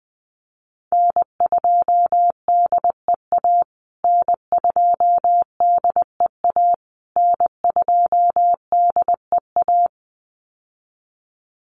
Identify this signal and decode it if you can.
Morse code, heard as a keyed tone, about 20 wpm, 700 Hz. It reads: N2DEA N2DEA N2DEA
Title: USA Call Signs Repeated 3x 20wpm